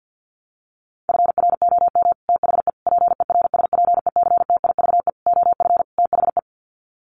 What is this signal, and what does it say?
Morse code, heard as a keyed tone, about 50 wpm, 700 Hz. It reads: FROM THE PERSPECTIVE OF THE
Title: Sets of 5 Words Encore2 50wpm